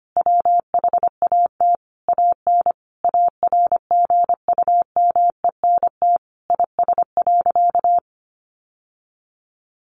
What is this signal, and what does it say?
Morse code, heard as a keyed tone, about 25 wpm, 700 Hz. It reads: WHAT AN ARGUMENT IS.
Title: Sentences from Top 1400 Words 25wpm